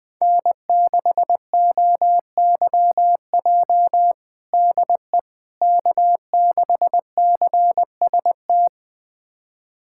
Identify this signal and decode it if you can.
Morse code, heard as a keyed tone, about 21 wpm, 700 Hz. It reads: N6OYJ DE K6CST
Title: USA Callsign DE Callsign - 5 Characters - 20wpm